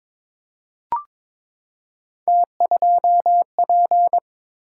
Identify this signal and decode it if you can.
Morse code, heard as a keyed tone, about 22 wpm, 700 Hz. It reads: T2P
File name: practice / 22wpm / Three Letter-Number Combinations - 22wpm